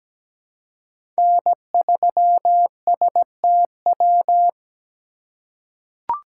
Morse code, heard as a keyed tone, about 17 wpm, 700 Hz. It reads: N3STW
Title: USA Call Signs - 5 Characters - 3x Character Spacing - 17wpm